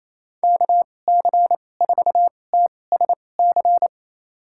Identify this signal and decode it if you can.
Morse code, heard as a keyed tone, about 28 wpm, spaced as for 20 wpm, 700 Hz. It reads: KC4TSC
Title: USA Call Signs - 6 Characters - 2x Character Spacing - 28wpm